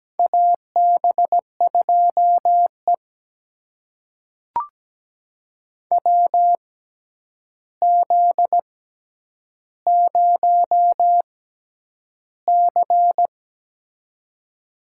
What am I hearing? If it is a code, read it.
Morse code, heard as a keyed tone, about 17 wpm, spaced as for 9 wpm, 700 Hz. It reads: AB2E WZ0C
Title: USA Call Signs - 4 Characters - 6x Character Spacing - 17wpm